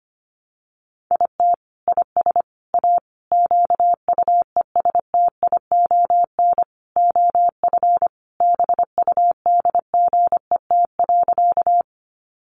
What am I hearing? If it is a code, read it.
Morse code, heard as a keyed tone, about 25 wpm, 700 Hz. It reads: IT IS A QUESTION OF BUDGET.
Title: Sentences from Top 1200 Words 25wpm